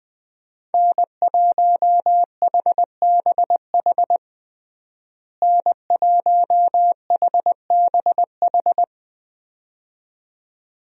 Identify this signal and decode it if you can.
Morse code, heard as a keyed tone, about 20 wpm, 700 Hz. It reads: N1HBH N1HBH
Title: USA Call Signs - 5 Characters - Repeated 2x - 20wpm